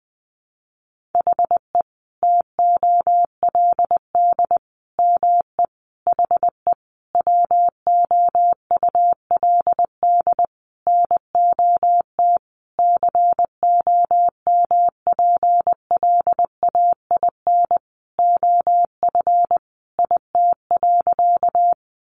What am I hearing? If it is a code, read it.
Morse code, heard as a keyed tone, about 20 wpm, 700 Hz. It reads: HE TOLD ME HE WOULD NOT COMPLAIN OF IT.
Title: Sentences from Top 1300 Words 20wpm